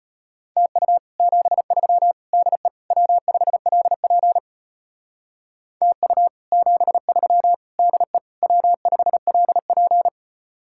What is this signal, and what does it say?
Morse code, heard as a keyed tone, about 33 wpm, 700 Hz. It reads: TU 73 DE W5LP TU 73 DE W5LP